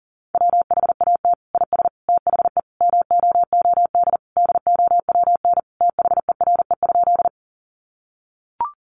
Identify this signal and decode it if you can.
Morse code, heard as a keyed tone, about 40 wpm, 700 Hz. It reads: WHAT IS THE MOOD DOWN THERE?
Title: Sentences from Top 1100 Words 40wpm